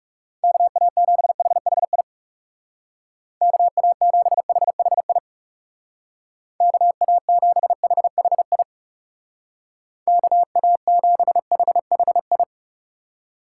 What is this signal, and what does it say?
Morse code, heard as a keyed tone, about 36 wpm, 700 Hz. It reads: KA7HHI KA7HHI KA7HHI KA7HHI
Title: Speed Racing - USA Call Signs 30wpm